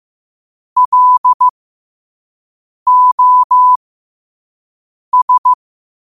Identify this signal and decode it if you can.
Morse code, heard as a keyed tone, about 15 wpm, spaced as for 5 wpm, 1000 Hz. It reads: LOS
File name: qotdfiles / 5wpm